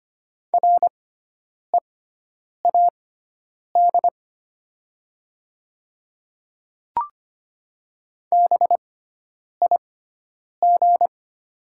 Morse code, heard as a keyed tone, about 25 wpm, spaced as for 9 wpm, 700 Hz. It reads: READ BIG
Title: 6x Character Spacing - Top 200 Words - 25wpm